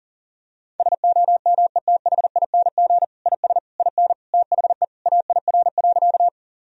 Morse code, heard as a keyed tone, about 40 wpm, 700 Hz. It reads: SOMETHING IS IN THE AIR.
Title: Sentences from Top 1000 Words - Review Part 1 - 40wpm